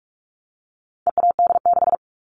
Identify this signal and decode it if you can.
Morse code, heard as a keyed tone, about 44 wpm, 700 Hz. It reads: EAD6
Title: All Cut Numbers - 4-Digit 45wpm